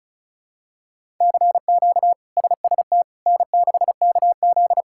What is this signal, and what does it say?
Morse code, heard as a keyed tone, about 34 wpm, 700 Hz. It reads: CQ SST N6KZ
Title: K1USN SST QSO Practice - 35wpm